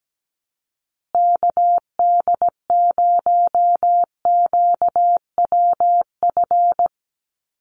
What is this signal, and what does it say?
Morse code, heard as a keyed tone, about 17 wpm, 700 Hz. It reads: KD0QWF
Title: USA Call Signs Repeated 3x 17wpm